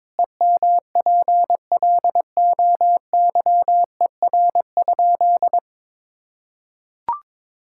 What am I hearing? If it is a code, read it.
Morse code, heard as a keyed tone, about 22 wpm, 700 Hz. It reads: EMPLOYER?
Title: Sentences from Top 1600 Words 22wpm